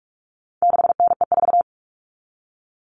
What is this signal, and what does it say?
Morse code, heard as a keyed tone, about 44 wpm, 700 Hz. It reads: 6NE4
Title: All Cut Numbers - 4-Digit 45wpm